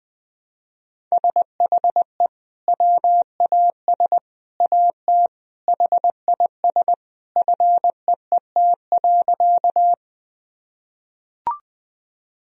Morse code, heard as a keyed tone, about 21 wpm, 700 Hz. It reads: SHE WAS AT HIS FEET.
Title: Sentences from Top 1000 Words - Review Part 1 - 20wpm